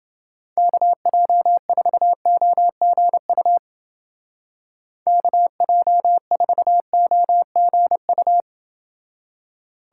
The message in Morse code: KJ4OGU KJ4OGU

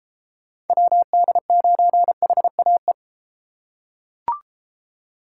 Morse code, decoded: WD9HAE

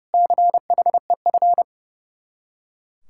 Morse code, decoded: CHEF

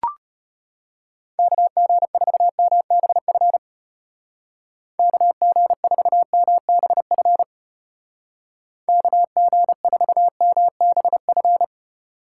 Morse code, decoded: KG4MBF KG4MBF KG4MBF

